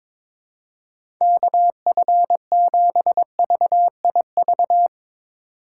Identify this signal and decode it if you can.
Morse code, heard as a keyed tone, about 22 wpm, 700 Hz. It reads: KF7VIV
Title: USA Call Signs - 6 Characters - 4x Character Spacing - 22wpm